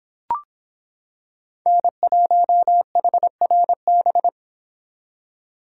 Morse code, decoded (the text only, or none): N1HRB